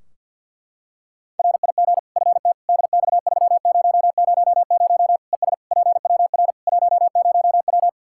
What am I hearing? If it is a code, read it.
Morse code, heard as a keyed tone, about 50 wpm, 700 Hz. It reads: RIG FT DX3000 ES PWR 10W
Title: Practice QSO - Exchange 2 - Info - 50wpm